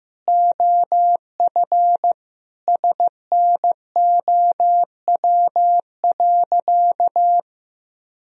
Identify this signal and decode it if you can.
Morse code, heard as a keyed tone, about 15 wpm, 700 Hz. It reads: OF SNOW.
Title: Sentences from Top 1000 Words - Review Part 3 - 15wpm